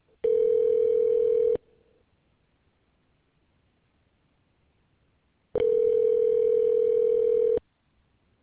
Ringback tone (North America), 440 + 480 Hz.